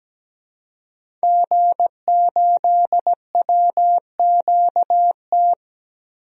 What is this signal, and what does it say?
Morse code, heard as a keyed tone, about 17 wpm, 700 Hz. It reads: G8WQT
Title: DXCC-100 Call Signs - 5 Characters - Repeated 2x - 17wpm